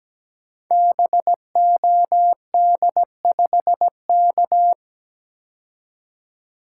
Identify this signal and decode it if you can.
Morse code, heard as a keyed tone, about 17 wpm, 700 Hz. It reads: BOD5K
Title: Code Groups - 5 Letter:Numbers - 17wpm